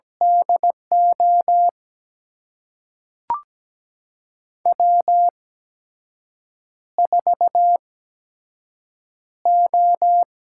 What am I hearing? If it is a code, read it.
Morse code, heard as a keyed tone, about 17 wpm, spaced as for 6 wpm, 700 Hz. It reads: DO W4O
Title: USA Call Signs - 5 Characters - 8x Character Spacing - 17wpm